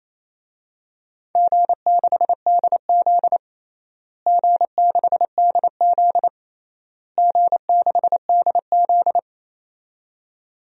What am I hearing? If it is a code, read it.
Morse code, heard as a keyed tone, about 27 wpm, 700 Hz. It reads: G6DZ G6DZ G6DZ
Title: DXCC-100 Call Signs - 4 Characters - Repeated 3x - 28wpm